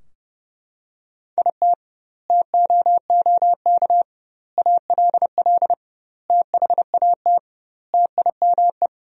Morse code, heard as a keyed tone, about 30 wpm, 700 Hz. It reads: IT TOOK ALL THAT TIME
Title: 2x Word-Spacing - Sentences from Top 300 Words - 30wpm